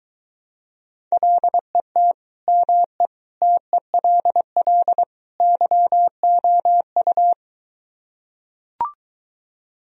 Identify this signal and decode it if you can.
Morse code, heard as a keyed tone, about 23 wpm, 700 Hz. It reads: LET ME TELL YOU